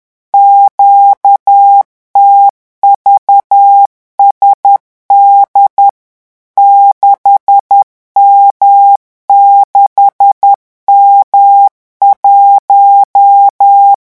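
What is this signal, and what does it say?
Morse code, heard as a keyed tone, about 10 wpm, 800 Hz. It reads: QTVSD 6M6M1
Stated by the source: Morse revision exercise for French radio amateurs at 10 words per minute.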